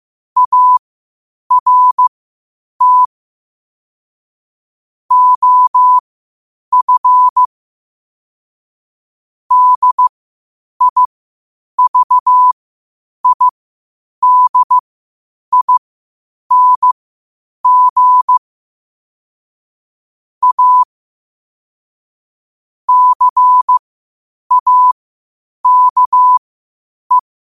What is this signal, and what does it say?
Morse code, heard as a keyed tone, about 15 wpm, spaced as for 8 wpm, 1000 Hz. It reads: ART OF DIVIDING A CAKE
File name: qotdfiles / 8wpm